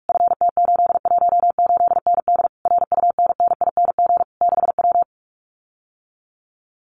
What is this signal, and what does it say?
Morse code, heard as a keyed tone, about 46 wpm, 700 Hz. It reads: FT818ND RUNNING 6W
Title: Common QSO Phrases Encore 45wpm